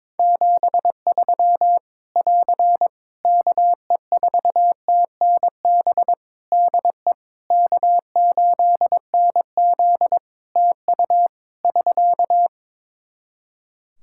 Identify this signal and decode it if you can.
Morse code, heard as a keyed tone, about 21 wpm, 700 Hz. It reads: 73 <AR> KE4TNB DE K8NZ TU <SK>
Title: Practice QSO - Exchange 4 - Ending - 22wpm